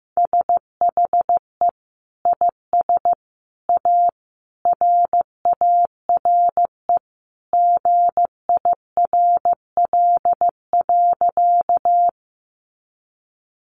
Morse code, heard as a keyed tone, about 15 wpm, 700 Hz. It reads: SHE IS A RARE GIRL.
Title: Sentences from Top 1100 Words 15wpm